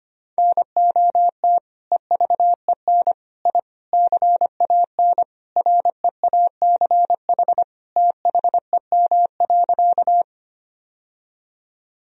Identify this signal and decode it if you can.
Morse code, heard as a keyed tone, about 25 wpm, 700 Hz. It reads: NOT EVEN I CAN REACH THEM.
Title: Sentences from Top 1000 Words - Review Part 2 - 25wpm